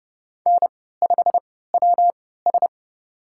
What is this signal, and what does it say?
Morse code, heard as a keyed tone, about 32 wpm, spaced as for 17 wpm, 700 Hz. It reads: N5WS
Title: USA Call Signs - 4 Characters - 3x Character Spacing - 30wpm